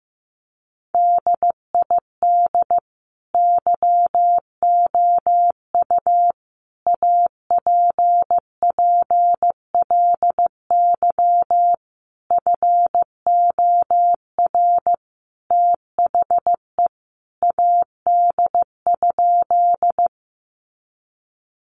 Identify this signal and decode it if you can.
Morse code, heard as a keyed tone, about 15 wpm, 700 Hz. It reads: DID YOU APPLY FOR THE AD?
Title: Sentences from Top 1300 Words 15wpm